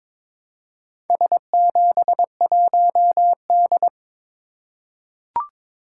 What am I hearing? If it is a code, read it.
Morse code, heard as a keyed tone, about 22 wpm, 700 Hz. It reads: S71D
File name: practice / 22wpm / Four Letter-Number Combinations - 22wpm